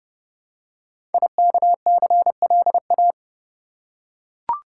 Morse code, heard as a keyed tone, about 30 wpm, 700 Hz. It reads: IKCLA
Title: Code Groups - 5 Letter:Numbers - 30wpm